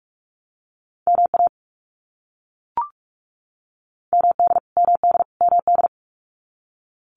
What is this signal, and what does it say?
Morse code, heard as a keyed tone, about 44 wpm, 700 Hz. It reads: MA MD MD MD
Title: US States Repeated 3x 45wpm